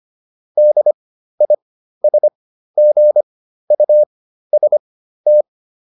Morse code, heard as a keyed tone, about 25 wpm, spaced as for 13 wpm, 600 Hz. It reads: DISGUST